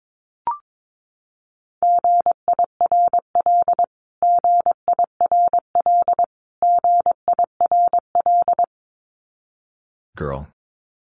Morse code, heard as a keyed tone, about 22 wpm, 700 Hz. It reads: GIRL GIRL GIRL
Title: Top 500 Words Repeated 22wpm